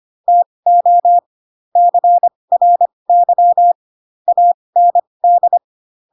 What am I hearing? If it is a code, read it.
Morse code, heard as a keyed tone, about 25 wpm, spaced as for 20 wpm, 700 Hz. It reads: TO CRY AND